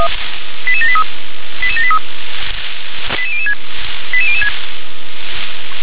Five-tone selective call (ZVEI), heard as quick runs of five tones, tones of about 70 ms. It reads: 88783 88783 88006 88006
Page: /357/SDR/ZVEI/